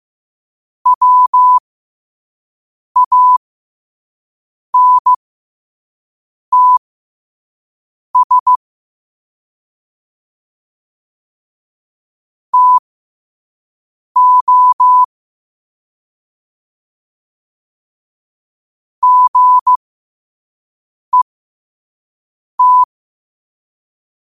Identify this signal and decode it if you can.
Morse code, heard as a keyed tone, about 15 wpm, spaced as for 5 wpm, 1000 Hz. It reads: WANTS TO GET
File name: qotdfiles / 5wpm